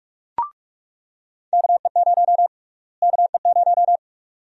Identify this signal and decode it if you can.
Morse code, heard as a keyed tone, about 46 wpm, 700 Hz. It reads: KE0 KE0
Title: USA Call Sign Prefixes - Repeated 2x - 45wpm